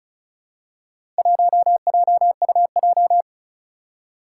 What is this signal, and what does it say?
Morse code, heard as a keyed tone, about 36 wpm, 690 Hz. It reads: 1JUJ
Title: Four Challenging Letter-Number Combinations 35wpm